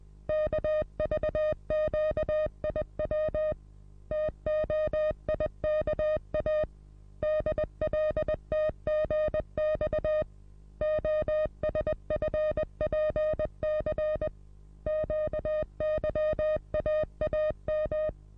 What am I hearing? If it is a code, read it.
Morse code, heard as a keyed tone, about 20 wpm, 620 Hz. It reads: KVQIW TOIKA DLTGX OSFPC QYAAM
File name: repos / Morsekurs / DJ4UF Morsen27